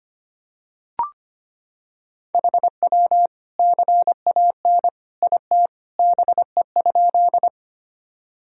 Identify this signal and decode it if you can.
Morse code, heard as a keyed tone, about 25 wpm, 700 Hz. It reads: HW CAN IT BE?
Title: Sentences from Top 100 Words and Abbreviations 25wpm